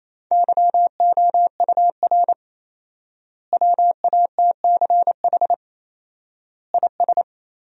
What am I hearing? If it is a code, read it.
Morse code, heard as a keyed tone, about 28 wpm, 700 Hz. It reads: YOUR WATCH IS
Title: 4x Word-Spacing - Sentences from Top 400 Words - 28wpm